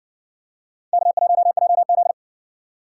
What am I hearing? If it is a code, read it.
Morse code, heard as a keyed tone, about 56 wpm, 700 Hz. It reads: K1JZ